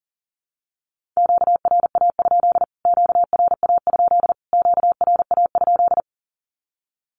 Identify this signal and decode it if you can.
Morse code, heard as a keyed tone, about 40 wpm, 700 Hz. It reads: QRA? QRA? QRA?
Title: Q-Codes Repeated 3x 40wpm